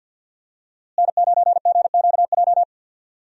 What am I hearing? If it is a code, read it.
Morse code, heard as a keyed tone, about 52 wpm, 700 Hz. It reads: N9GQJ
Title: USA Call Signs - 5 Characters - Repeated 3x - 50wpm